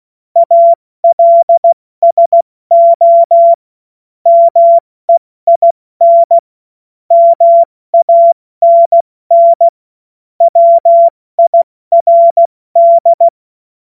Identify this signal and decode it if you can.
Morse code, heard as a keyed tone, about 16 wpm, 680 Hz. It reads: ALSO MEIN MANN WIRD